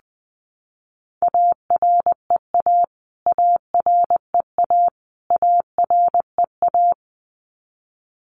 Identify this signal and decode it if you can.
Morse code, heard as a keyed tone, about 20 wpm, 700 Hz. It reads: AREA AREA AREA
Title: Top 500 Words Repeated 20wpm